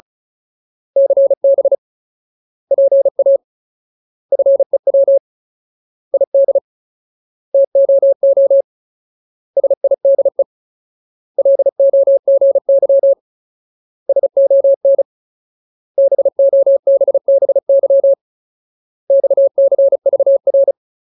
Morse code, heard as a keyed tone, about 34 wpm, 550 Hz. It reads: CB PA FEW ID TOO SIDE LOGY SON BOBBY XCVR
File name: Ultimate Word List / 35WPM Ultimate